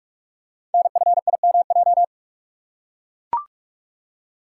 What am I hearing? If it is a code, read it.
Morse code, heard as a keyed tone, about 44 wpm, 690 Hz. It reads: NFIMJ
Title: Code Groups - 5 Letters - 45wpm